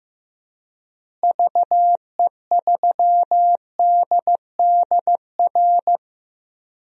Morse code, heard as a keyed tone, about 15 wpm, 700 Hz. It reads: VE3DDR
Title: DXCC-100 Call Signs - 6 Characters - Repeated 3x - 15wpm